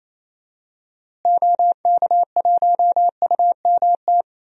Morse code, heard as a keyed tone, about 28 wpm, 700 Hz. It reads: OK1UMT
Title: DXCC-100 Call Signs - 6 Characters - Repeated 3x - 28wpm